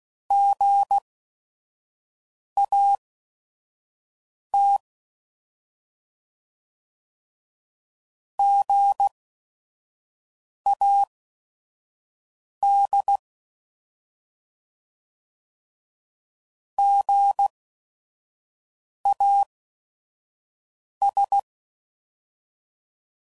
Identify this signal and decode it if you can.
Morse code, heard as a keyed tone, about 16 wpm, spaced as for 5 wpm, 800 Hz. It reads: GAT GAD GAS
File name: Prefixes / By First Letter / Gs